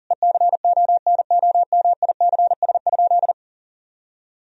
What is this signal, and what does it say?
Morse code, heard as a keyed tone, about 40 wpm, 700 Hz. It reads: ECONOMICS?
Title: Sentences from Top 1000 Words - Review Part 3 - 40wpm